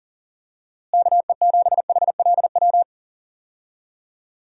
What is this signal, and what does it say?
Morse code, heard as a keyed tone, about 40 wpm, 700 Hz. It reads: KE7HLW
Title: USA Call Signs - 6 Characters - Repeated 2x - 40wpm